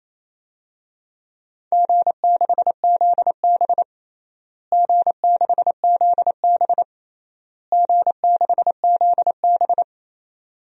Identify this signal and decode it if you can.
Morse code, heard as a keyed tone, about 28 wpm, 700 Hz. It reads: G6ZB G6ZB G6ZB
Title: DXCC-100 Call Signs - 4 Characters - Repeated 3x - 28wpm